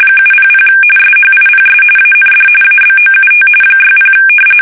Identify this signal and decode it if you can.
RTTY, about 45.45 baud. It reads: 700 SHIPS AND ABOUT 60
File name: fsk850 / usnavy3